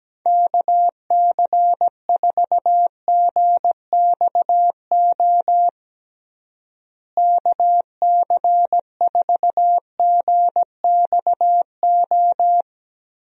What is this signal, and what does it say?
Morse code, heard as a keyed tone, about 17 wpm, 700 Hz. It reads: KC4GXO KC4GXO
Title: USA Call Signs - 6 Characters - Repeated 3x - 17wpm